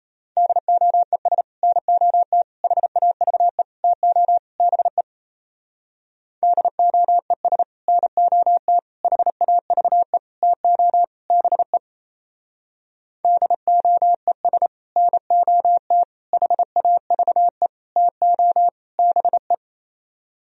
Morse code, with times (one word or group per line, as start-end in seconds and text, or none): 0.4-1.4 DOES
1.6-2.4 NOT
2.6-3.6 HAVE
3.8-4.4 TO
4.6-5.0 BE
6.4-7.6 DOES
7.9-8.8 NOT
9.0-10.2 HAVE
10.4-11.0 TO
11.3-11.8 BE
13.2-14.7 DOES
15.0-16.0 NOT
16.3-17.7 HAVE
18.0-18.7 TO
19.0-19.5 BE